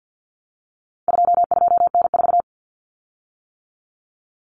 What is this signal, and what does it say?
Morse code, heard as a keyed tone, about 50 wpm, 700 Hz. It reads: 22N4
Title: Common Cut Numbers - 4-Digit 50wpm